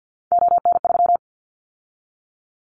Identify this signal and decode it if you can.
Morse code, heard as a keyed tone, about 50 wpm, 700 Hz. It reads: ON3
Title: Callsign Prefixes for Top 50 DXCC - 50wpm